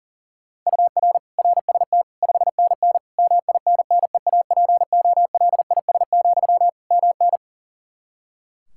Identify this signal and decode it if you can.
Morse code, heard as a keyed tone, about 37 wpm, 700 Hz. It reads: UR RST 5NN MINNEAPOLIS, MN
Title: Practice QSO - Exchange 1 - Beginning - 40wpm